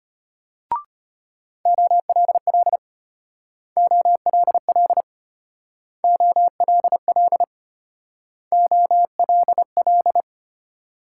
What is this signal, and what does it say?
Morse code, heard as a keyed tone, about 32 wpm, 700 Hz. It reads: OLL OLL OLL OLL